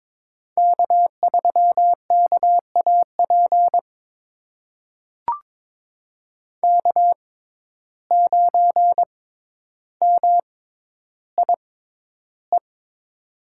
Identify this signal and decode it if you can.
Morse code, heard as a keyed tone, about 22 wpm, spaced as for 11 wpm, 700 Hz. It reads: K3KAP K9MIE
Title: USA Call Signs - 5 Characters - 6x Character Spacing - 22wpm